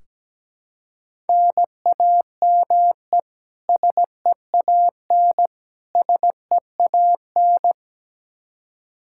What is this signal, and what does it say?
Morse code, heard as a keyed tone, about 17 wpm, 700 Hz. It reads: NAME SEAN SEAN
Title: Practice QSO - Exchange 1 - Beginning - 17wpm